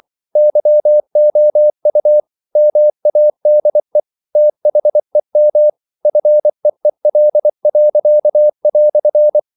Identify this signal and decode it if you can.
Morse code, heard as a keyed tone, about 24 wpm, 600 Hz. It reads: YOU MADE THEM FEEL."
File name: CW / Practice / EnglishQuotations0224